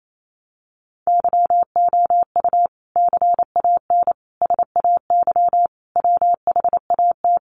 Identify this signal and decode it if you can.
Morse code, heard as a keyed tone, about 28 wpm, 700 Hz. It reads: YOU CAN SAY WHAT
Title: Sentences from Top 200 Words 28wpm